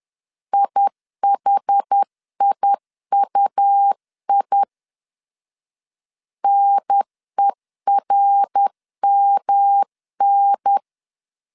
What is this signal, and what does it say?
Morse code, heard as a keyed tone, about 10 wpm, 800 Hz. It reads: IHIUI NERMN
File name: 10wpm